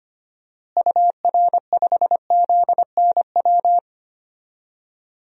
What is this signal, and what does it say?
Morse code, heard as a keyed tone, about 25 wpm, 700 Hz. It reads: UR5ZNW
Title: DXCC-100 Call Signs - 6 Characters - Repeated 3x - 25wpm